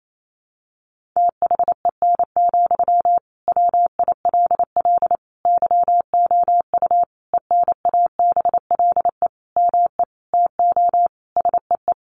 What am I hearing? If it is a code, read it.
Morse code, heard as a keyed tone, about 28 wpm, 700 Hz. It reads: THEN, WILL YOU ENABLE ME TO SEE
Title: Sentences from Top 1500 Words 28wpm